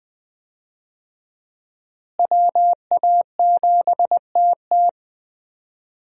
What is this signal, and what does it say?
Morse code, heard as a keyed tone, about 20 wpm, 700 Hz. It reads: WA7TT
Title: USA Call Signs - 5 Characters - Repeated 3x - 20wpm